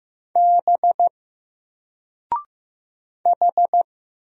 Morse code, heard as a keyed tone, about 15 wpm, 700 Hz. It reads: B H
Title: Single Letter-Number 15wpm